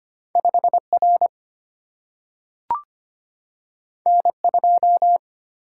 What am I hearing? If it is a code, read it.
Morse code, heard as a keyed tone, about 25 wpm, 700 Hz. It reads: 5 R N 2
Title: Single-Letter Single-Number 25wpm